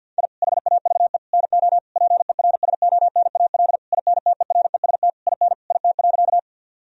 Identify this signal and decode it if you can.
Morse code, heard as a keyed tone, about 50 wpm, 700 Hz. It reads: I HAVE NO PERSONAL INTEREST IN IT.
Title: Sentences from Top 1000 Words - Review Part 4 - 50wpm